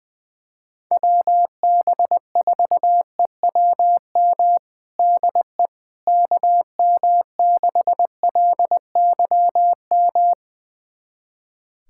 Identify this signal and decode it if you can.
Morse code, heard as a keyed tone, about 21 wpm, 700 Hz. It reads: WB4EWM DE KM6LYM
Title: USA Callsign DE Callsign 20wpm